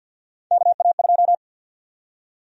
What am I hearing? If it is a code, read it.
Morse code, heard as a keyed tone, about 50 wpm, 700 Hz. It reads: KA2
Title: USA Call Sign Prefixes - Repeated 2x - 50wpm